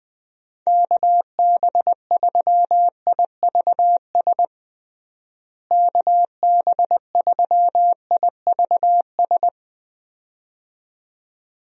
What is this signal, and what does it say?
Morse code, heard as a keyed tone, about 21 wpm, 700 Hz. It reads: KB3IVS KB3IVS
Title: USA Call Signs - 6 Characters - Repeated 2x - 20wpm